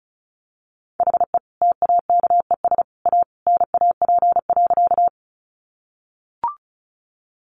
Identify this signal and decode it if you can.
Morse code, heard as a keyed tone, about 34 wpm, 700 Hz. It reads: HE TAKES A NAP.
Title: Easy Phrases 35wpm